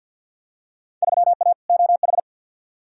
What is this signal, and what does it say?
Morse code, heard as a keyed tone, about 47 wpm, 700 Hz. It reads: 3A OH
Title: Field Day Exchanges - 50wpm